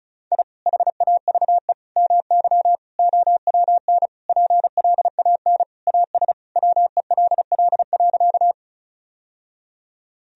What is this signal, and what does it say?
Morse code, heard as a keyed tone, about 36 wpm, 700 Hz. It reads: I HAVE MY OWN PLAN AS WELL.
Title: Sentences from Top 1000 Words - Review Part 2 - 35wpm